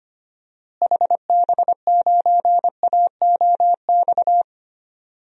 Morse code, heard as a keyed tone, about 25 wpm, 700 Hz. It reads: HB9AOX
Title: DXCC-100 Call Signs - 6 Characters - Repeated 2x - 25wpm